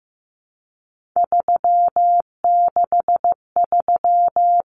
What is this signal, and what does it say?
Morse code, heard as a keyed tone, about 15 wpm, 700 Hz. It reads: 363
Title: Common Cut Numbers - 3-Digit 15wpm